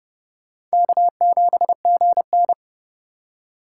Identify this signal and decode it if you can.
Morse code, heard as a keyed tone, about 34 wpm, 700 Hz. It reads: K7GN
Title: Speed Racing - USA 1x2 Call Signs 30wpm